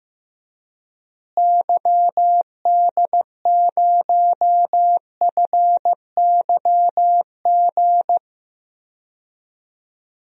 Morse code, heard as a keyed tone, about 15 wpm, 700 Hz. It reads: YD0FYG
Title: DXCC-100 Call Signs - 6 Characters - Repeated 2x - 15wpm